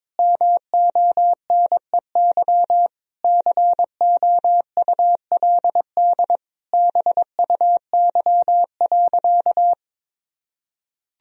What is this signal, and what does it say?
Morse code, heard as a keyed tone, about 22 wpm, 700 Hz. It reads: MONEY COULD BUY.
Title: Sentences from Top 1000 Words - Review Part 4 - 22wpm